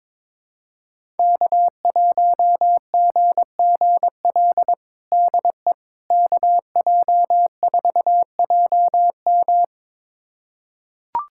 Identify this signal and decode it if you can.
Morse code, heard as a keyed tone, about 22 wpm, 700 Hz. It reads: K1GGL DE KJ4JM
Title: USA Callsign DE Callsign - 5 Characters - 22wpm